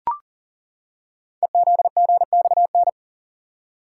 Morse code, heard as a keyed tone, about 41 wpm, 700 Hz. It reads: EZGXN
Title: Code Groups - 5 Letter:Numbers - 40wpm